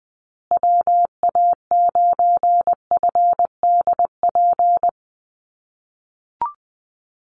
Morse code, heard as a keyed tone, about 20 wpm, 700 Hz. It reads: WA9FDP
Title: USA Call Signs Repeated 2x 20wpm